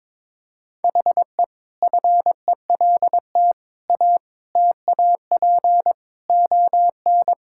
Morse code, heard as a keyed tone, about 22 wpm, 700 Hz. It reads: HE FELT A TAP ON
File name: practice / 22wpm / Sentences with 4-Word Phrases - 22wpm